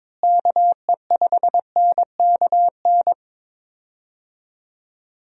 Morse code, heard as a keyed tone, about 22 wpm, 700 Hz. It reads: KE5NKN
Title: USA Call Signs - 6 Characters - Repeated 3x - 22wpm